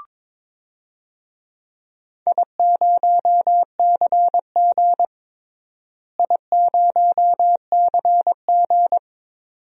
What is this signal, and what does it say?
Morse code, heard as a keyed tone, about 22 wpm, 700 Hz. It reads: I0CG I0CG
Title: DXCC-100 Call Signs - 4 Characters - Repeated 2x - 22wpm